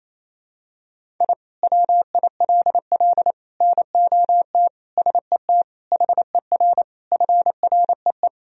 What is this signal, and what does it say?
Morse code, heard as a keyed tone, about 28 wpm, 700 Hz. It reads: I WILL NOT SET HER FREE
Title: Sentences from Top 1000 Words - Review Part 1 - 28wpm